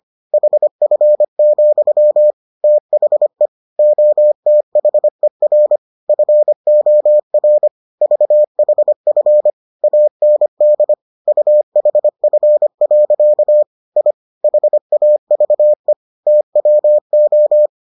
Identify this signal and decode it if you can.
Morse code, heard as a keyed tone, about 25 wpm, 600 Hz. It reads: HF, THE OTHER FOR VHF AND UHF. I HAVE TWO